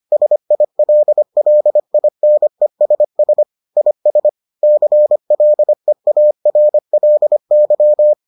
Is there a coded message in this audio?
Yes: Morse code from 0.1 to 8.1 s, heard as a keyed tone, about 25 wpm, 600 Hz. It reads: SILLINESS IS CLEARLY